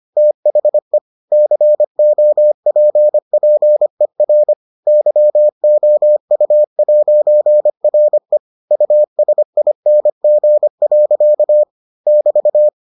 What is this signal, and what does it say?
Morse code, heard as a keyed tone, about 25 wpm, 600 Hz. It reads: THE COPPER YOU'RE USING. <BT>